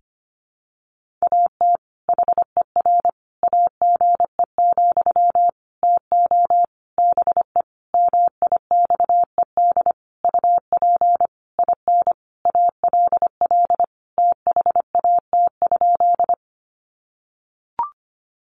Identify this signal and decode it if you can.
Morse code, heard as a keyed tone, about 25 wpm, 700 Hz. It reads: AT HER AGE, TO BE MIXED UP IN ALL THAT?
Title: Sentences from Top 1500 Words 25wpm